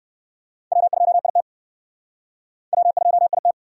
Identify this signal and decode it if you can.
Morse code, heard as a keyed tone, about 59 wpm, 700 Hz. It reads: W2IT W2IT